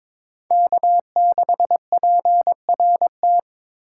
Morse code, heard as a keyed tone, about 26 wpm, 700 Hz. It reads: K6PRT
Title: Speed Racing - USA 1x3 Call Signs 22wpm